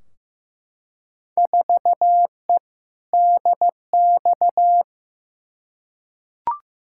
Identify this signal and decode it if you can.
Morse code, heard as a keyed tone, about 15 wpm, 700 Hz. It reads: 4E DX
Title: Field Day Exchanges Encore - 15wpm